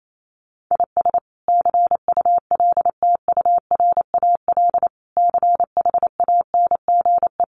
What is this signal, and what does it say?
Morse code, heard as a keyed tone, about 28 wpm, 700 Hz. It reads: IS CULTURAL CHANGE
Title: Sentences from Top 1500 Words 28wpm